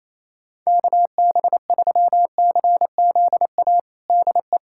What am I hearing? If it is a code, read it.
Morse code, heard as a keyed tone, about 28 wpm, 700 Hz. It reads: KB3CZA DE
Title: USA Callsign DE Callsign - 6 Characters - 28wpm